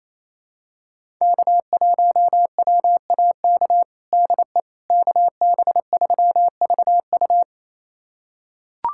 Morse code, heard as a keyed tone, about 28 wpm, 700 Hz. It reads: K1WAK DE KB3VU
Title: USA Callsign DE Callsign - 5 Characters - 28wpm